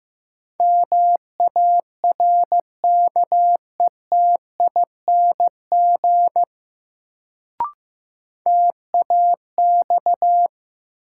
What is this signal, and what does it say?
Morse code, heard as a keyed tone, about 15 wpm, 700 Hz. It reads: MARKETING TAX
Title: Sentences from Top 1000 Words - Review Part 5 - New Word Drill - 15wpm